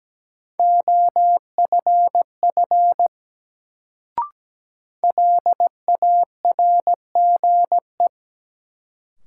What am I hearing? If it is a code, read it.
Morse code, heard as a keyed tone, about 17 wpm, 700 Hz. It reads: OFF LARGE
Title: Sentences from Top 200 Words - New Word Drill - 17wpm